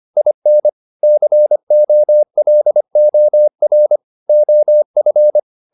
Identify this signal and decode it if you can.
Morse code, heard as a keyed tone, about 25 wpm, 600 Hz. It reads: IN COLOR OF
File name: the old man and the sea part 1 25wpm